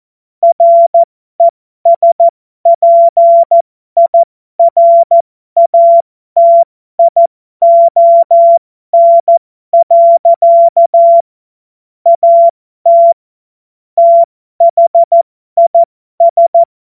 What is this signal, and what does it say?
Morse code, heard as a keyed tone, about 14 wpm, spaced as for 12 wpm, 680 Hz. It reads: RESPIRATION. AT THIS